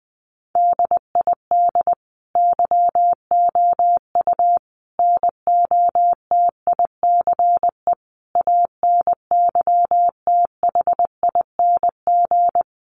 Morse code, heard as a keyed tone, about 20 wpm, 700 Hz. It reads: DID YOU NOTICE ANYTHING
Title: Sentences from Top 1000 Words 20wpm